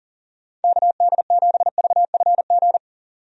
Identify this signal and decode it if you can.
Morse code, heard as a keyed tone, about 40 wpm, 700 Hz. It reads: KD7VFG
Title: USA Call Signs - 6 Characters - Repeated 2x - 40wpm